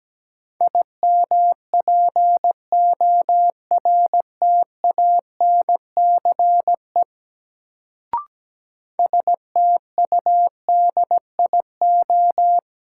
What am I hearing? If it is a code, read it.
Morse code, heard as a keyed tone, about 17 wpm, 700 Hz. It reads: IMPORTANCE STUDIO
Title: Sentences from Top 1200 Words - New Word Drill Alt - 17wpm